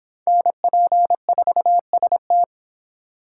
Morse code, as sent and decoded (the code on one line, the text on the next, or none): -. .--. ....- ... -
NP4ST